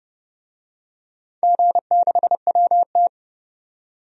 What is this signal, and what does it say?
Morse code, heard as a keyed tone, about 32 wpm, 700 Hz. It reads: G6WT
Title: DXCC-100 Call Signs - 4 Characters - 30wpm